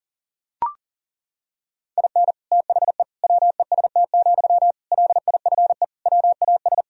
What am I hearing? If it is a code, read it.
Morse code, heard as a keyed tone, about 37 wpm, 700 Hz. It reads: IN THE WEST, LIFE WAS
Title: Sentences from Top 1000 Words - Review Part 2 - 40wpm